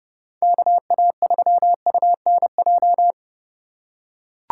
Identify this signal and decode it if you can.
Morse code, heard as a keyed tone, about 27 wpm, 700 Hz. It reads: KA3UNJ